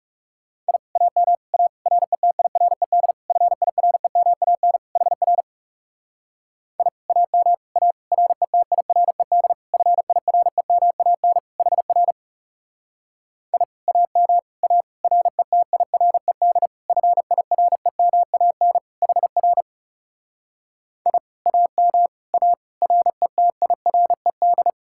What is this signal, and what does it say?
Morse code, heard as a keyed tone, about 39 wpm, 700 Hz. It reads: I AM A RETIRED FIREMAN HR I AM A RETIRED FIREMAN HR I AM A RETIRED FIREMAN HR I AM A RETIRED